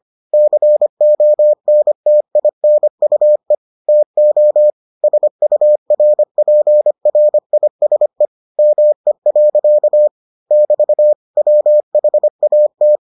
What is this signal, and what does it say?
Morse code, heard as a keyed tone, about 25 wpm, 600 Hz. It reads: CONTINUE TO SURPRISE ME. <BT> WHAT